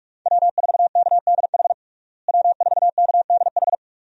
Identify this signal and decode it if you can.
Morse code, heard as a keyed tone, about 46 wpm, 700 Hz. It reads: W4KDH W4KDH
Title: USA Call Signs - 5 Characters - Repeated 2x - 45wpm